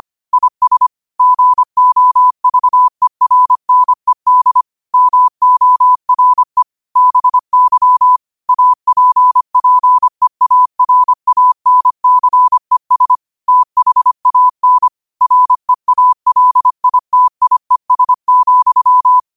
Morse code, heard as a keyed tone, about 25 wpm, 1000 Hz. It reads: IS GOVERNED MORE BY APPEARANCES THAN REALITIES,
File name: qotdfiles / 25wpm